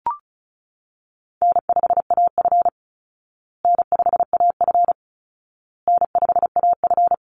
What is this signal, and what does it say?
Morse code, heard as a keyed tone, about 34 wpm, 700 Hz. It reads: N5AF N5AF N5AF
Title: Easy Call Signs Repeated 3x - 35wpm